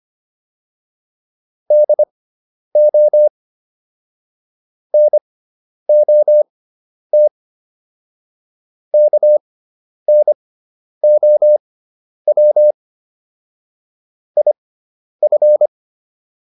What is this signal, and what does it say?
Morse code, heard as a keyed tone, about 25 wpm, spaced as for 10 wpm, 600 Hz. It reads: DO NOT KNOW IF